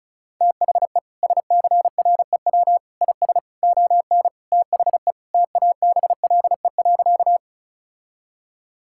Morse code, heard as a keyed tone, about 34 wpm, 700 Hz. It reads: THE SCREW IS ON THE TABLE.
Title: Sentences from Top 1600 Words 35wpm